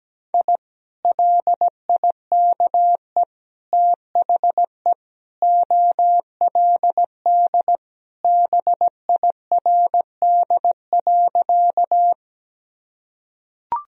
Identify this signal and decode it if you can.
Morse code, heard as a keyed tone, about 17 wpm, 700 Hz. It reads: I LIKE THE OLD BIRD.
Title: Sentences from Top 1000 Words - Review Part 2 - 17wpm